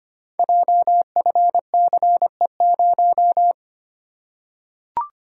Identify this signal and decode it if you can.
Morse code, heard as a keyed tone, about 25 wpm, 700 Hz. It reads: JFCE0
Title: Code Groups - 5 Letter:Numbers - 25wpm